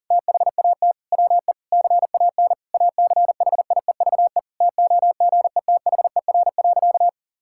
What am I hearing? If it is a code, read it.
Morse code, heard as a keyed tone, about 40 wpm, 700 Hz. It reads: THAT WE CAN ACHIEVE TOGETHER.
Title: Sentences from Top 1000 Words - Review Part 6 - 40wpm